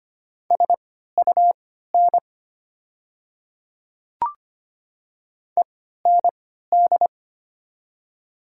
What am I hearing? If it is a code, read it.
Morse code, heard as a keyed tone, about 25 wpm, spaced as for 14 wpm, 700 Hz. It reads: SUN END
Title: 3x Character Spacing - Top 300 Words - 25wpm